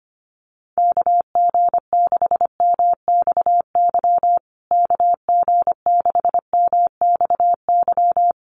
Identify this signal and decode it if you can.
Morse code, heard as a keyed tone, about 25 wpm, 700 Hz. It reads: KG6MXY KG6MXY
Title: USA Call Signs Repeated 3x 25wpm